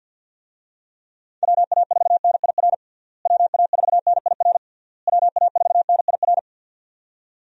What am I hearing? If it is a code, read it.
Morse code, heard as a keyed tone, about 50 wpm, 700 Hz. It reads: WA4NIR WA4NIR WA4NIR
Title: USA Call Signs - 6 Characters - Repeated 3x - 50wpm